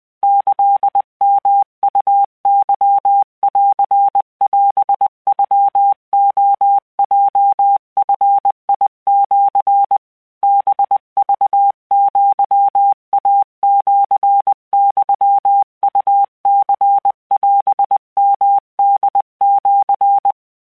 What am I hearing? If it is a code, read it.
Morse code (Japanese Wabun code), heard as a keyed tone, about 20 wpm, 800 Hz. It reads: キヨウケンオノレヲヂシ ハクアイシユウニオヨホシ
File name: keuikuchiyokugo20wptan